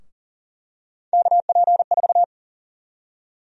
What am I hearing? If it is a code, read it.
Morse code, heard as a keyed tone, about 37 wpm, 700 Hz. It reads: KP4
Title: USA Call Sign Prefixes - 40wpm